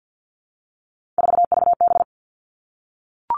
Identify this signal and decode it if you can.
Morse code, heard as a keyed tone, about 50 wpm, 700 Hz. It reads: 4VB
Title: Three Challenging Letter-Number Combinations 50wpm